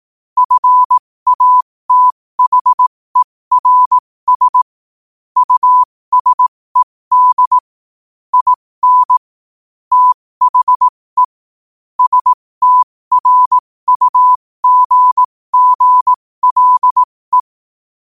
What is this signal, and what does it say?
Morse code, heard as a keyed tone, about 18 wpm, spaced as for 15 wpm, 1000 Hz. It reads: FATHERS USED IN THE STRUGGLE
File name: qotdfiles / 15wpm